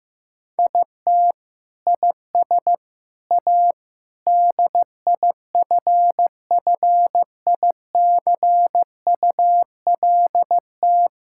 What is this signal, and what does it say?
Morse code, heard as a keyed tone, about 15 wpm, 700 Hz. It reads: IT IS A DIFFICULT
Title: Sentences from Top 1000 Words - Review Part 6 - 15wpm